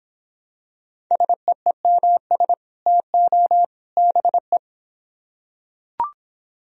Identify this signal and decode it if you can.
Morse code, heard as a keyed tone, about 24 wpm, 700 Hz. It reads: SEEMS TO BE